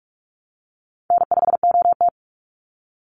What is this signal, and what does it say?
Morse code, heard as a keyed tone, about 44 wpm, 700 Hz. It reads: N5OT
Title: Easy Call Signs Repeated 2x - 45wpm